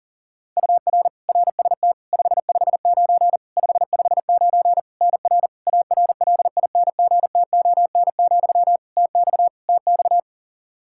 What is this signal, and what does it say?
Morse code, heard as a keyed tone, about 40 wpm, 700 Hz. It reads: UR RST 559 559 NR ARLINGTON, TX TX
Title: Practice QSO - Exchange 1 - Beginning - 40wpm